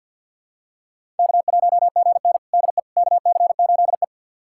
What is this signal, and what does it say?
Morse code, heard as a keyed tone, about 47 wpm, 700 Hz. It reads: K1GN DE KC8E
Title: USA Callsign DE Callsign - 4 Characters - 50wpm